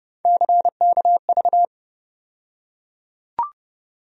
Morse code, heard as a keyed tone, about 30 wpm, 700 Hz. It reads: CKV
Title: USA Call Sign Postfixes - 3 Letters 30wpm